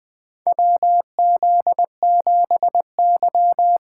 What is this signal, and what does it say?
Morse code, heard as a keyed tone, about 20 wpm, 700 Hz. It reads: WZ7Y
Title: USA Call Signs - 3x Character Spacing - 20wpm